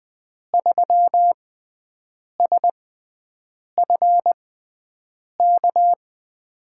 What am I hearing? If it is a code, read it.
Morse code, heard as a keyed tone, about 20 wpm, spaced as for 7 wpm, 700 Hz. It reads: 3SFK
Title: USA Call Signs - 6 Characters - 6x Character Spacing - 20wpm